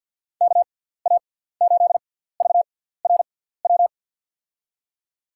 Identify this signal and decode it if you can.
Morse code, heard as a keyed tone, about 53 wpm, spaced as for 17 wpm, 700 Hz. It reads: KA8VRW
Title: USA Call Signs - 6 Characters - 6x Character Spacing - 50wpm